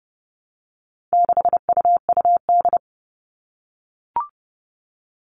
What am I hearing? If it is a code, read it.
Morse code, heard as a keyed tone, about 30 wpm, 700 Hz. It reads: 6UUD
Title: All Cut Numbers - 4-Digit 30wpm